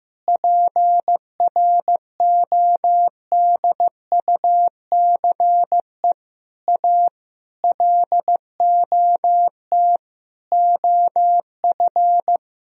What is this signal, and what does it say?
Morse code, heard as a keyed tone, about 15 wpm, 700 Hz. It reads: PRODUCE A LOT OF
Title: Sentences from Top 1000 Words - Review Part 3 - 15wpm